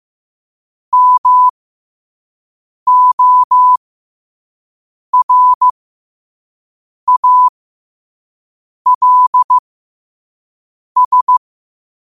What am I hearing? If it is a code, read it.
Morse code, heard as a keyed tone, about 15 wpm, spaced as for 5 wpm, 1000 Hz. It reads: MORALS
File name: qotdfiles / 5wpm